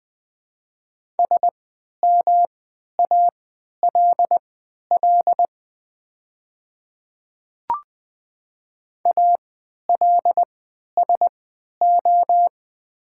Morse code, heard as a keyed tone, about 20 wpm, spaced as for 11 wpm, 700 Hz. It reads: SMALL ALSO
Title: 3x Character Spacing - Top 300 Words - 20wpm